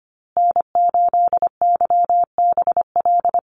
Morse code, heard as a keyed tone, about 25 wpm, 700 Hz. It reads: N8YBL
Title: USA Call Signs Repeated 3x 25wpm